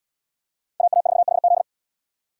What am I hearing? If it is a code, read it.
Morse code, heard as a keyed tone, about 67 wpm, 700 Hz. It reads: AA4HZ